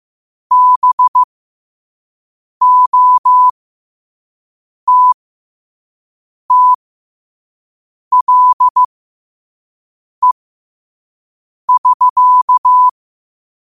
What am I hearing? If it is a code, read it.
Morse code, heard as a keyed tone, about 15 wpm, spaced as for 5 wpm, 1000 Hz. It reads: BOTTLE<SK>
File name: 5wpm